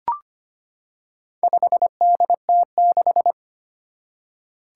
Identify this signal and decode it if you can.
Morse code, heard as a keyed tone, about 25 wpm, 700 Hz. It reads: 5DT6
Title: Four Challenging Letter-Number Combinations 25wpm